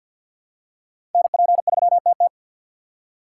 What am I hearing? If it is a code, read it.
Morse code, heard as a keyed tone, about 49 wpm, 700 Hz. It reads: NP3TT
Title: USA Call Signs - 6x Character Spacing - 50wpm